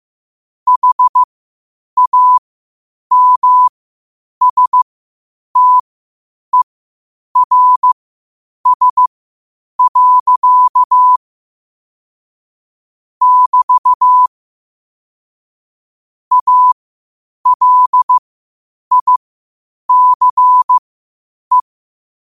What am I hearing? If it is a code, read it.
Morse code, heard as a keyed tone, about 15 wpm, spaced as for 8 wpm, 1000 Hz. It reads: HAMSTERS. <BT> ALICE
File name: 8wpm